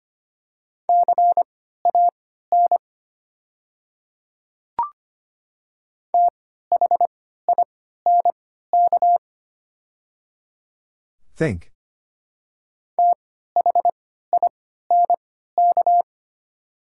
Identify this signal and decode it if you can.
Morse code, heard as a keyed tone, about 25 wpm, spaced as for 14 wpm, 700 Hz. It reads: CAN THINK THINK
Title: 3x Character Spacing - Top 200 Words - 25wpm